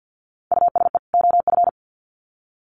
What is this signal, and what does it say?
Morse code, heard as a keyed tone, about 50 wpm, 700 Hz. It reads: USE OF
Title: Sets of 2 Words 50wpm